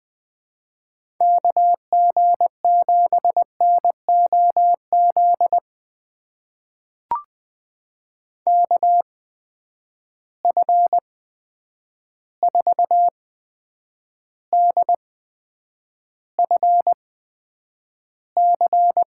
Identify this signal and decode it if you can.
Morse code, heard as a keyed tone, about 21 wpm, spaced as for 9 wpm, 700 Hz. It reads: KG7NOZ KF4DFC
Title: USA Call Signs - 6 Characters - 8x Character Spacing - 20wpm